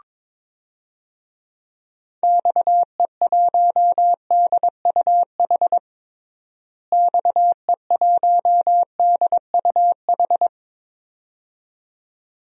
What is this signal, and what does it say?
Morse code, heard as a keyed tone, about 22 wpm, 700 Hz. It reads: XE1DUH XE1DUH
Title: DXCC-100 Call Signs - 6 Characters - Repeated 2x - 22wpm